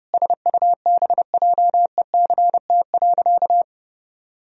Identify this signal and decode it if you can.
Morse code, heard as a keyed tone, about 30 wpm, 700 Hz. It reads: SUBJECT.
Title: Sentences from Top 1000 Words - Review Part 1 - 30wpm